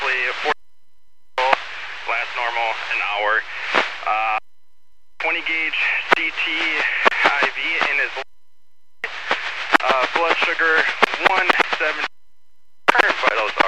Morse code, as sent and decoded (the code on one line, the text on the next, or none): none